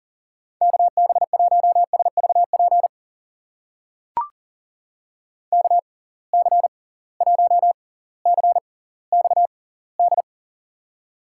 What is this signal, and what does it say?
Morse code, heard as a keyed tone, about 40 wpm, spaced as for 20 wpm, 700 Hz. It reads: KB1SVP KC1CXD
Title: USA Call Signs - 6 Characters - 6x Character Spacing - 40wpm